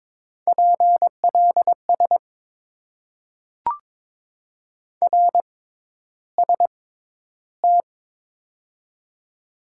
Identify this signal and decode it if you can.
Morse code, heard as a keyed tone, about 22 wpm, spaced as for 11 wpm, 700 Hz. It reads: PLS RST
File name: Top 90 QSO Elements - 6x Character Spacing - 22wpm